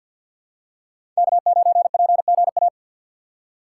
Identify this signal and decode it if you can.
Morse code, heard as a keyed tone, about 52 wpm, 700 Hz. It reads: K9PGA
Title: USA Call Signs - 5 Characters - 6x Character Spacing - 50wpm